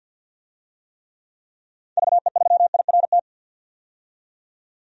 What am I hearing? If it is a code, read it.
Morse code, heard as a keyed tone, about 49 wpm, 700 Hz. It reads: VE3IRT
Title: DXCC-100 Call Signs - 6 Characters - 50wpm